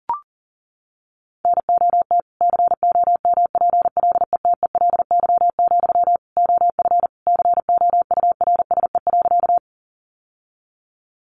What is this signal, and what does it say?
Morse code, heard as a keyed tone, about 40 wpm, 700 Hz. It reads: NOT COMPLETELY, OF COURSE.
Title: Sentences from Top 1100 Words 40wpm